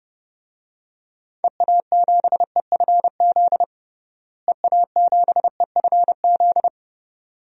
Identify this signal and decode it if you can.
Morse code, heard as a keyed tone, about 32 wpm, 700 Hz. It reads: EA7EFZ EA7EFZ
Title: DXCC-100 Call Signs - 6 Characters - Repeated 2x - 30wpm